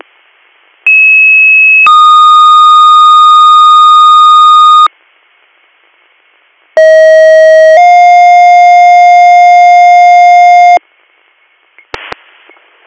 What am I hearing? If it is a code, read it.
Two-tone page, heard as a one-second tone, then a three-second tone, about 2600 then 1230 Hz, about 640 then 710 Hz. It is heard from 0.9 to 10.8 s.